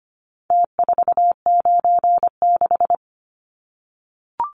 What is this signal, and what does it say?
Morse code, heard as a keyed tone, about 25 wpm, 700 Hz. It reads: T496
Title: Numbers - Contest Style - 4-Digit 25wpm